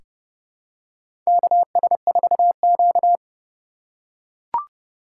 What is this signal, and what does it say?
Morse code, heard as a keyed tone, about 30 wpm, 700 Hz. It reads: KS4Q
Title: USA Call Signs - 4 Characters - 2x Character Spacing - 30wpm